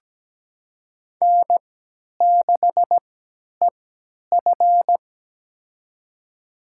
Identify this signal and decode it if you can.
Morse code, heard as a keyed tone, about 17 wpm, spaced as for 10 wpm, 700 Hz. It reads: N6EF
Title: USA Call Signs - 4 Characters - 3x Character Spacing - 17wpm